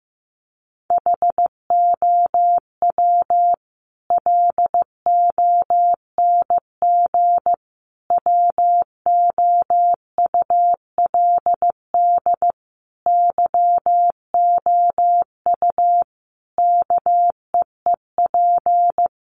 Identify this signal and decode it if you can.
Morse code, heard as a keyed tone, about 15 wpm, 700 Hz. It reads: HOW LONG WOULD YOU KEEP
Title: Sentences from Top 500 Words 15wpm